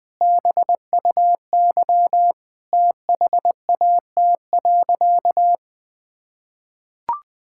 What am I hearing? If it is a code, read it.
Morse code, heard as a keyed tone, about 20 wpm, 700 Hz. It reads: BUY THAT.
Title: Sentences with 5-Word Phrases - 20wpm